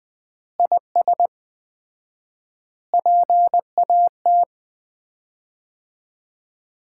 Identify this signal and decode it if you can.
Morse code, heard as a keyed tone, about 20 wpm, 700 Hz. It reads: IS PAT
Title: Common QSO Phrases - 4x Word-Spacing - 20wpm